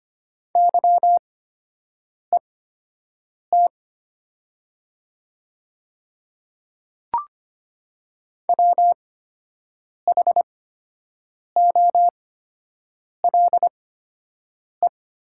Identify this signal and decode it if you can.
Morse code, heard as a keyed tone, about 25 wpm, spaced as for 7 wpm, 700 Hz. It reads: YET WHOLE
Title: 8x Character Spacing - Top 500 Words - 25wpm